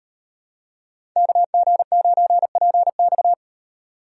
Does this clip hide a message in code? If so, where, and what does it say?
Yes: Morse code from 1.2 to 3.3 s, heard as a keyed tone, about 38 wpm, 700 Hz. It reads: KG9PX